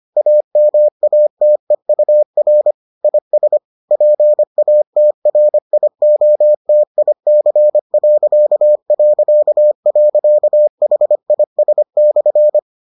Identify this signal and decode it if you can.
Morse code, heard as a keyed tone, about 25 wpm, 600 Hz. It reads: AMATEUR IS PATRIOTIC...HIS/